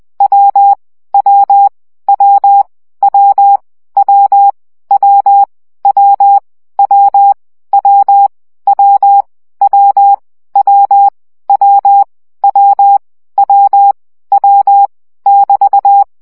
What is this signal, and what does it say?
Morse code, heard as a keyed tone, about 20 wpm, 800 Hz. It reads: W W W W W W W W W W W W W W W W <BT>
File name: CW gyakorlas w 17 baud